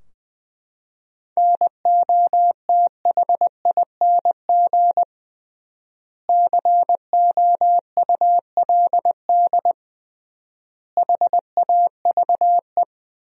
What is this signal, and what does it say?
Morse code, heard as a keyed tone, about 20 wpm, 700 Hz. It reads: NOTHING COULD HAVE
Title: 3x Word-Spacing - Sentences from Top 500 Words - 20wpm